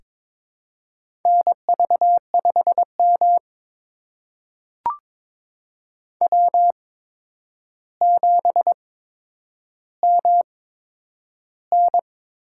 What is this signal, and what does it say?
Morse code, heard as a keyed tone, about 22 wpm, spaced as for 9 wpm, 700 Hz. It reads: NV5M W7MN
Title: USA Call Signs - 4 Characters - 8x Character Spacing - 22wpm